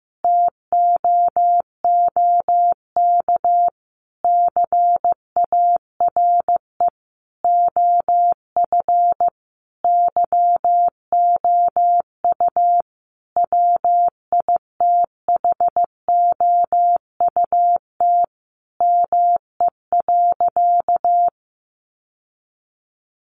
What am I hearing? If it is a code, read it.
Morse code, heard as a keyed tone, about 15 wpm, 700 Hz. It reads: TOOK CARE OF YOU WITHOUT ME.
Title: Sentences from Top 1200 Words 15wpm